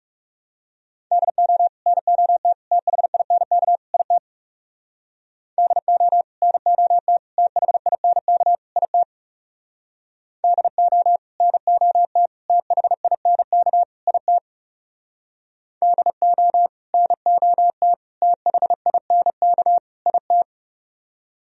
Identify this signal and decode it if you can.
Morse code, heard as a keyed tone, about 37 wpm, 700 Hz. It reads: DO NOT THINK IT DO NOT THINK IT DO NOT THINK IT DO NOT THINK IT
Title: Speed-Racing - Sets of 4 Words 30wpm